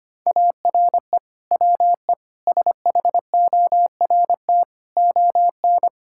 Morse code, heard as a keyed tone, about 25 wpm, 700 Hz. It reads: ARE WE SHORT ON
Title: Sentences from Top 1000 Words - Review Part 5 - 25wpm